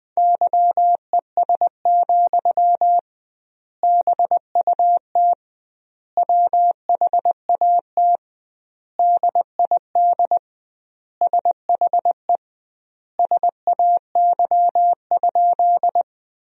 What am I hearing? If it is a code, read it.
Morse code, heard as a keyed tone, about 21 wpm, 700 Hz. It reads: YES, BUT WHAT DID SHE SAY?
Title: 2x Word-Spacing - Sentences from Top 500 Words - 20wpm